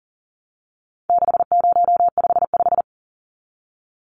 Morse code, heard as a keyed tone, about 40 wpm, 700 Hz. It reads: 6055
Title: Numbers - 4-Digit 40wpm